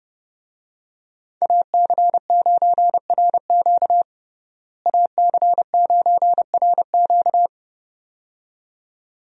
Morse code, heard as a keyed tone, about 30 wpm, 700 Hz. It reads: AC9RQ AC9RQ
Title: USA Call Signs - 5 Characters - Repeated 2x - 30wpm